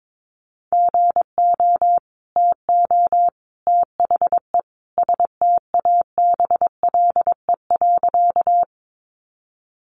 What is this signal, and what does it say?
Morse code, heard as a keyed tone, about 22 wpm, 700 Hz. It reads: GO TO THE STABLE.
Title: Sentences from Top 1500 Words 22wpm